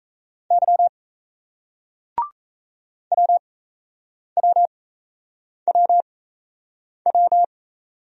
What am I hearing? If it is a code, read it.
Morse code, heard as a keyed tone, about 38 wpm, spaced as for 8 wpm, 700 Hz. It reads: Y WWWW